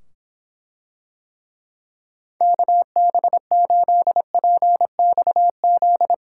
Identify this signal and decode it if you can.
Morse code, heard as a keyed tone, about 24 wpm, 700 Hz. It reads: KB8PXZ